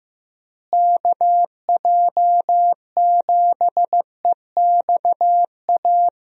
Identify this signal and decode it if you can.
Morse code, heard as a keyed tone, about 15 wpm, 700 Hz. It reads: KJ7EXA
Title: USA Call Signs - 6 Characters - Repeated 2x - 15wpm